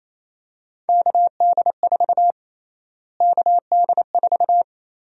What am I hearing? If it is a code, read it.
Morse code, heard as a keyed tone, about 27 wpm, 700 Hz. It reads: KD4 KD4
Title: USA Call Sign Prefixes - Repeated 2x - 28wpm